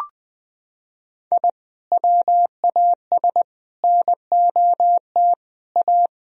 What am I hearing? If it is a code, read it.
Morse code, heard as a keyed tone, about 20 wpm, 700 Hz. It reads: I WAS NOT A
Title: Sentences from Top 1000 Words - Review Part 4 - 20wpm